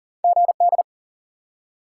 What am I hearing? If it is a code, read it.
Morse code, heard as a keyed tone, about 40 wpm, 700 Hz. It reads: GD
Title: Top 90 QSO Elements 40wpm